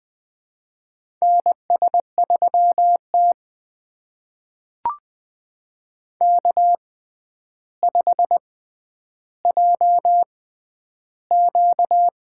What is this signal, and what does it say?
Morse code, heard as a keyed tone, about 20 wpm, spaced as for 10 wpm, 700 Hz. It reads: NS3T K5JQ
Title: USA Call Signs - 4 Characters - 6x Character Spacing - 20wpm